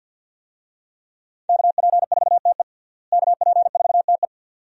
Morse code, heard as a keyed tone, about 52 wpm, 700 Hz. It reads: KP4TE KP4TE
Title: DXCC-100 Call Signs - 5 Characters - Repeated 2x - 50wpm